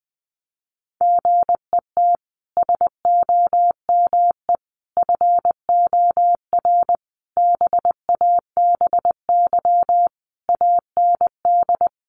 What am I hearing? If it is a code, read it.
Morse code, heard as a keyed tone, about 19 wpm, 700 Hz. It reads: GET SOME FOR BABY AND
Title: Sentences from Top 1300 Words 20wpm